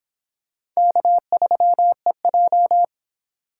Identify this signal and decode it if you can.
Morse code, heard as a keyed tone, about 24 wpm, 700 Hz. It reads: K3EJ